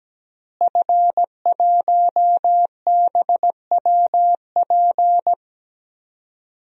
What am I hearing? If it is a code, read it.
Morse code, heard as a keyed tone, about 17 wpm, 700 Hz. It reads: F1BWP
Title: DXCC-100 Call Signs - 5 Characters - Repeated 2x - 17wpm